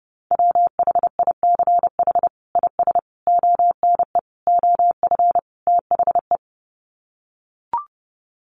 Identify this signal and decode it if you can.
Morse code, heard as a keyed tone, about 30 wpm, 700 Hz. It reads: WHICH IS ONE OF THE
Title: Sets of 5 Words Encore 30wpm